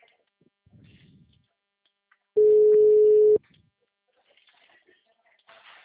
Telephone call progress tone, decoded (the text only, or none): ringback tone